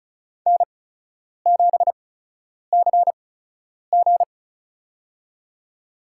Morse code, heard as a keyed tone, about 36 wpm, spaced as for 10 wpm, 700 Hz. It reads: N7CG
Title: USA Call Signs - 8x Character Spacing - 35wpm